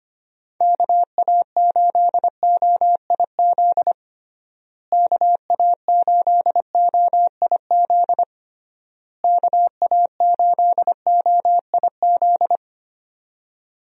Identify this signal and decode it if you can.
Morse code, heard as a keyed tone, about 25 wpm, 700 Hz. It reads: KA8OIZ KA8OIZ KA8OIZ
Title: USA Call Signs - 6 Characters - Repeated 3x - 25wpm